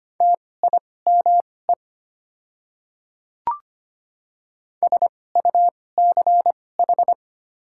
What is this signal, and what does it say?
Morse code, heard as a keyed tone, about 25 wpm, spaced as for 18 wpm, 690 Hz. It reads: TIME SUCH
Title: 2x Character Spacing - Top 200 Words - 25wpm